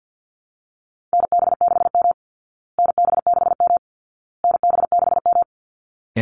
Morse code, heard as a keyed tone, about 50 wpm, 700 Hz. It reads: NB6M NB6M NB6M
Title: Call Signs Repeated 2x - 50wpm